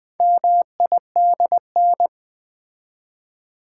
Morse code, heard as a keyed tone, about 20 wpm, 700 Hz. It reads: MIDN
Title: Build A Word - 2 Letters At A Time - 8-Letter Words - 20wpm